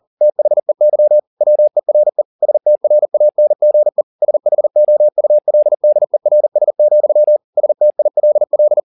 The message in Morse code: THEY WERE STRANGE SHOULDERS, STILL

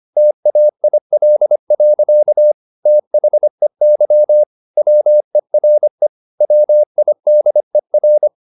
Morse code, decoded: TAIL. THEY WERE WIDER

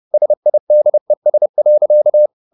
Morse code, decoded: SIDES.

